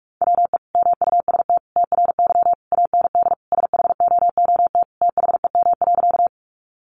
Morse code, heard as a keyed tone, about 44 wpm, 700 Hz. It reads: WE MUST TRY AND SHOOT THEM.
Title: Sentences from Top 1200 Words 45wpm